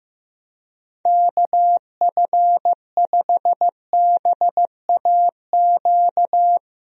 Morse code, heard as a keyed tone, about 15 wpm, 700 Hz. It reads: KF5BAQ
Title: USA Call Signs - 6 Characters - Repeated 2x - 15wpm